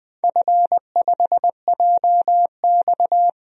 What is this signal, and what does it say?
Morse code, heard as a keyed tone, about 21 wpm, 700 Hz. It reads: F5JX
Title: DXCC-100 Call Signs - 4 Characters - 20wpm